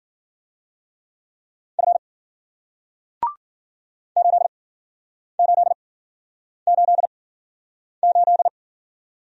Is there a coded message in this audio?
Yes: Morse code from 1.8 to 8.5 s, heard as a keyed tone, about 55 wpm, 700 Hz. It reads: F 8 8 8 8